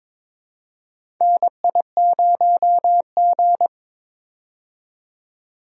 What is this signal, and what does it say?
Morse code, heard as a keyed tone, about 22 wpm, 700 Hz. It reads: NI0G
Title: USA Call Signs - 4 Characters - 22wpm